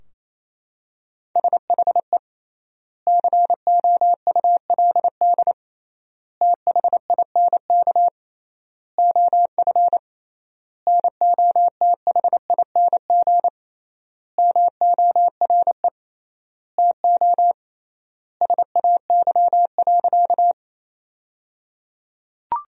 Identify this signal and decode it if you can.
Morse code, heard as a keyed tone, about 27 wpm, 700 Hz. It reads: SHE COULD THINK OF NOTHING MORE TO SAY.
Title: 3x Word-Spacing - Sentences from Top 500 Words - 28wpm